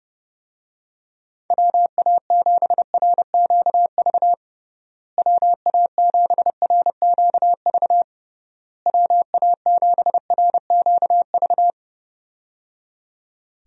Morse code, heard as a keyed tone, about 30 wpm, 700 Hz. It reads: WA7RQV WA7RQV WA7RQV
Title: USA Call Signs - 6 Characters - Repeated 3x - 30wpm